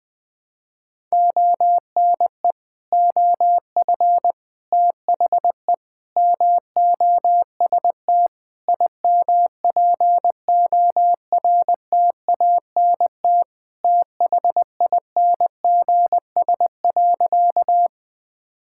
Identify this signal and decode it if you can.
Morse code, heard as a keyed tone, about 20 wpm, 700 Hz. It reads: ONE OF THE MOST IMPORTANT THINGS.
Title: Sentences with 4-Word Phrases - Spoken Beginning - 20wpm